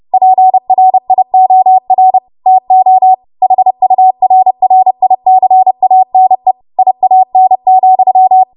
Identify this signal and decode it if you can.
Morse code, heard as a keyed tone, about 30 wpm, 750 Hz. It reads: PRIOR TO HURRICANE IAN,
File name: Morse / 30WPM